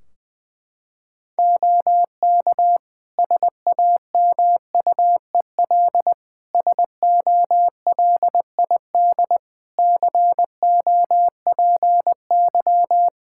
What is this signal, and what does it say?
Morse code, heard as a keyed tone, about 20 wpm, 700 Hz. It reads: OK SAMUEL SOLID COPY
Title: K1USN SST QSO Practice 20wpm